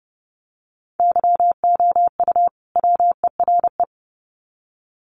Morse code, heard as a keyed tone, about 30 wpm, 700 Hz. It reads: YOU WERE
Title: Sets of 2 Words 30wpm